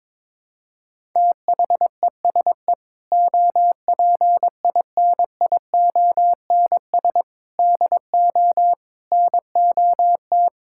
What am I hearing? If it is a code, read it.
Morse code, heard as a keyed tone, about 22 wpm, 700 Hz. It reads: THESE OPINIONS DO NOT
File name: practice / 22wpm / Sentences with 8-Word Phrases - 22wpm